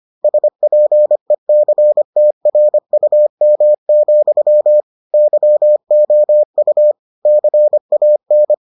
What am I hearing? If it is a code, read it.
Morse code, heard as a keyed tone, about 25 wpm, 600 Hz. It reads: SPECTRUM, YOU CAN